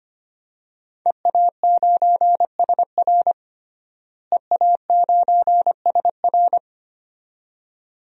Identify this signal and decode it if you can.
Morse code, heard as a keyed tone, about 25 wpm, 700 Hz. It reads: EA9SR EA9SR
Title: DXCC-100 Call Signs - 5 Characters - Repeated 2x - 25wpm